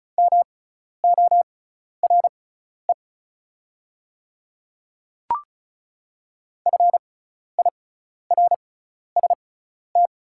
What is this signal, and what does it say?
Morse code, heard as a keyed tone, about 34 wpm, spaced as for 12 wpm, 690 Hz. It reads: MORE FIRST
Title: 6x Character Spacing - Top 100 Words - 35wpm